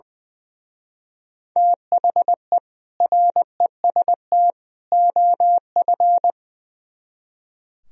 Morse code, heard as a keyed tone, about 23 wpm, 700 Hz. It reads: THE REST OF
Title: Speed-Racing - Sets of 3 Words 20wpm